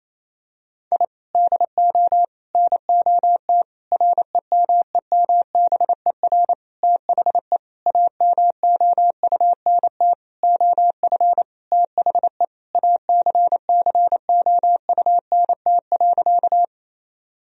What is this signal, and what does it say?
Morse code, heard as a keyed tone, about 28 wpm, 700 Hz. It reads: I DO NOT REMEMBER THE AMOUNT OF THE ACCOUNT.
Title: Sentences from Top 1000 Words - Review Part 4 - 28wpm